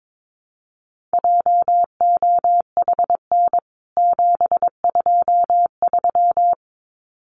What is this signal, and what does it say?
Morse code, heard as a keyed tone, about 22 wpm, 700 Hz. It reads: JOHN 723
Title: CWT Exchange 22wpm